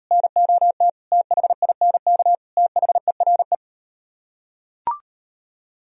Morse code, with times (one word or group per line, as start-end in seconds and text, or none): 0.1-0.9 NOT
1.1-2.3 THINK
2.6-3.5 THERE